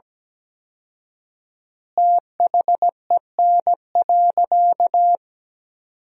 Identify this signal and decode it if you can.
Morse code, heard as a keyed tone, about 17 wpm, 700 Hz. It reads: THEN.
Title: 4x Word-Spacing - Sentences from Top 200 Words - 17wpm